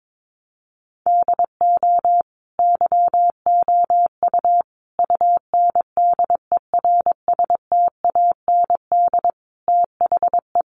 Morse code, heard as a keyed tone, about 22 wpm, 700 Hz. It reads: DO YOU UNDERSTAND THE
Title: Sentences from Top 1200 Words 22wpm